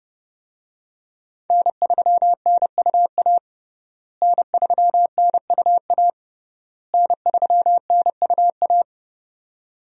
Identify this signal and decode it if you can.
Morse code, heard as a keyed tone, about 30 wpm, 700 Hz. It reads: N3NUA N3NUA N3NUA
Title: USA Call Signs - 5 Characters - Repeated 3x - 30wpm